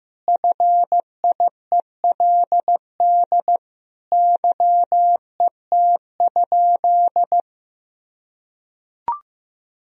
Morse code, heard as a keyed tone, about 15 wpm, 700 Hz. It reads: FIELD YET?
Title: Sentences from Top 1000 Words - Review Part 2 - 15wpm